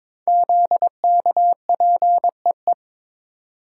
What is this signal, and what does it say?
Morse code, heard as a keyed tone, about 22 wpm, 700 Hz. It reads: ZKPEE
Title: Code Groups - 5 Letters - 22wpm